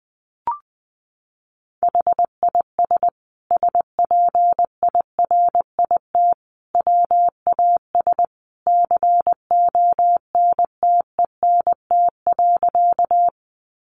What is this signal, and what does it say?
Morse code, heard as a keyed tone, about 20 wpm, 700 Hz. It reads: HIS SPIRIT WAS CONTENT.
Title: Sentences from Top 1400 Words 20wpm